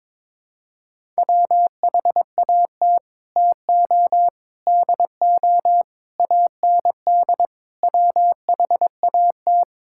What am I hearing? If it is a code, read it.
Morse code, heard as a keyed tone, about 22 wpm, 700 Hz. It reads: WHAT TO DO AND WHAT
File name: practice / 22wpm / Sets of 8 Words - 22wpm